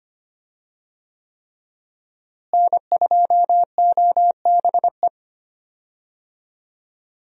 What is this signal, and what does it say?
Morse code, heard as a keyed tone, about 25 wpm, 700 Hz. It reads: N2OBE
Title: USA Call Signs - 5 Characters - Repeated 3x - 25wpm